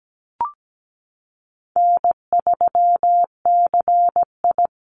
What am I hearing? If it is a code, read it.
Morse code, heard as a keyed tone, about 17 wpm, 700 Hz. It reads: N3CI
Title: Easy Call Signs - 17wpm